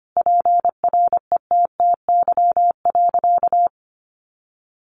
Morse code, heard as a keyed tone, about 25 wpm, 700 Hz. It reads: PRETTY.
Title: Easy Phrases 25wpm